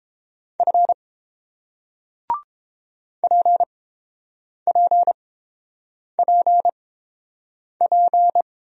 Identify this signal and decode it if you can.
Morse code, heard as a keyed tone, about 30 wpm, spaced as for 7 wpm, 700 Hz. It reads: F PPPP